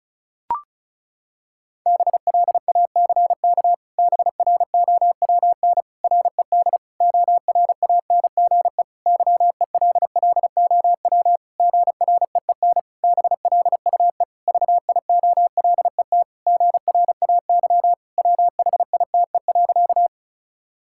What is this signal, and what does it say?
Morse code, heard as a keyed tone, about 34 wpm, 700 Hz. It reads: BLACK BROWN RED ORANGE YELLOW GREEN BLUE VIOLET GRAY WHITE.
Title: Pangram Challenge v2 - 35wpm